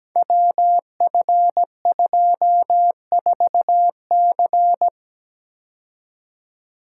Morse code, heard as a keyed tone, about 17 wpm, 700 Hz. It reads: WF24C
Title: Code Groups - 5 Letter:Numbers - 17wpm